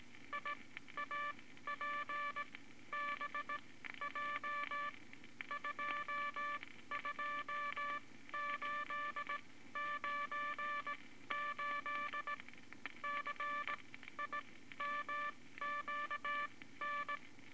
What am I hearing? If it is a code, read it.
Morse code, heard as a keyed tone, about 18 wpm, spaced as for 13 wpm, 1170 Hz, with noise. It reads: IAPBJ 22898 CIMQN
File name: Morse Code Records / 18 B LCM 165 Ten Words Per Minute